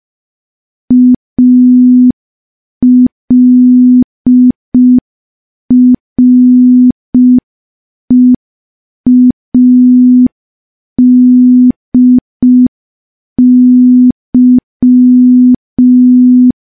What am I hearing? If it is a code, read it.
Morse code, heard as a keyed tone, about 5 wpm, 250 Hz. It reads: ALREADY
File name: morse / morse-training-05WPM